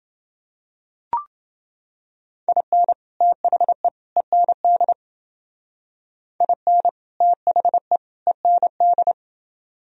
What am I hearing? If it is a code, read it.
Morse code, heard as a keyed tone, about 28 wpm, 700 Hz. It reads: IN THE END IN THE END